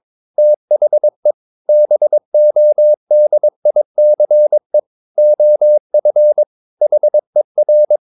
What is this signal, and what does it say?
Morse code, heard as a keyed tone, about 22 wpm, 600 Hz. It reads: THE BODICE OF HER
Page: Romanen Never af Ken Follet på engelsk.